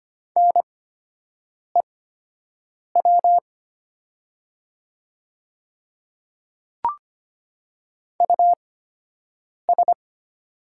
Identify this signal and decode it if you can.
Morse code, heard as a keyed tone, about 25 wpm, spaced as for 7 wpm, 700 Hz. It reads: NEW US
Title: 8x Character Spacing - Top 500 Words - 25wpm